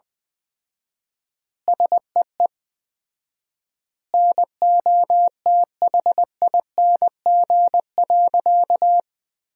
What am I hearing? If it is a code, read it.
Morse code, heard as a keyed tone, about 20 wpm, 700 Hz. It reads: SEE NOTHING.
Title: 4x Word-Spacing - Sentences from Top 500 Words - 20wpm